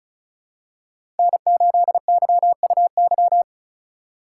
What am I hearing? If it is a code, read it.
Morse code, heard as a keyed tone, about 36 wpm, 700 Hz. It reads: N8YUY
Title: USA Call Signs - 5 Characters - 4x Character Spacing - 35wpm